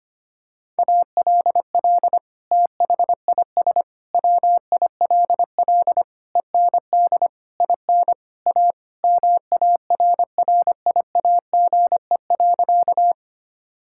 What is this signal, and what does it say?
Morse code, heard as a keyed tone, about 25 wpm, 700 Hz. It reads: ALL THIS WILL END IN A MARRIAGE.
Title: Sentences from Top 1000 Words - Review Part 5 - 25wpm